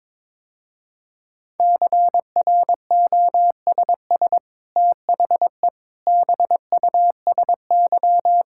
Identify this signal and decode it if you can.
Morse code, heard as a keyed tone, about 23 wpm, 700 Hz. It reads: CROSS THE BUSY
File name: practice / 22wpm / Easy Sentences with Hints - 22wpm